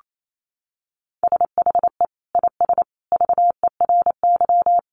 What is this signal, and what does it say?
Morse code, heard as a keyed tone, about 28 wpm, 700 Hz. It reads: SHE IS VERY
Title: Sentences from Top 1400 Words 28wpm